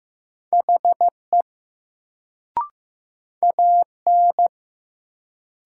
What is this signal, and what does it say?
Morse code, heard as a keyed tone, about 15 wpm, 700 Hz. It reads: HE AN
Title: Sets of 2 Words - New Word Drill - 15wpm